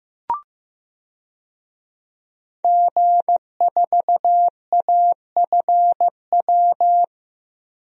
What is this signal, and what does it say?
Morse code, heard as a keyed tone, about 15 wpm, 700 Hz. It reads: G4AFW
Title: DXCC-100 Call Signs - 5 Characters - Repeated 3x - 15wpm